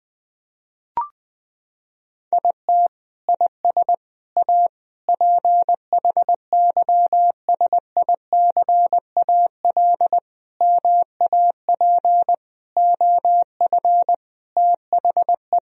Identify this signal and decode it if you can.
Morse code, heard as a keyed tone, about 20 wpm, 700 Hz. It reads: IT IS A PHYSICAL MAP OF THE
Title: Sentences from Top 1000 Words - Review Part 4 - 20wpm